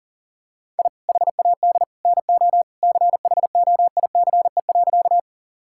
Morse code, heard as a keyed tone, about 40 wpm, 700 Hz. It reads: I HAD NO CHOICE.
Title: Sentences from Top 1000 Words - Review Part 6 - 40wpm